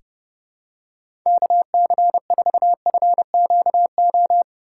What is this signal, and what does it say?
Morse code, heard as a keyed tone, about 30 wpm, 700 Hz. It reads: KC4FQO
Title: USA Call Signs - 6 Characters - 2x Character Spacing - 30wpm